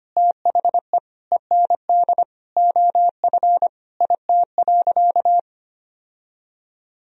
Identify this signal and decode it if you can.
Morse code, heard as a keyed tone, about 25 wpm, 700 Hz. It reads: THE END OF IT.
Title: Sentences from Top 1000 Words - Review Part 1 - 25wpm